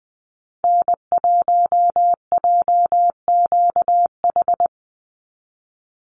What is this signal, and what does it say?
Morse code, heard as a keyed tone, about 20 wpm, 700 Hz. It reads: N1JQH
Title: USA Call Signs 20wpm